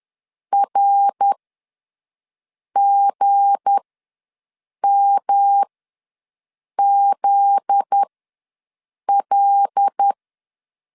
Morse code, heard as a keyed tone, about 10 wpm, spaced as for 5 wpm, 800 Hz. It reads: RGMZL
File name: LEZIONI / 7 GF9Z / 7 GF9Z ws 10wpm 3919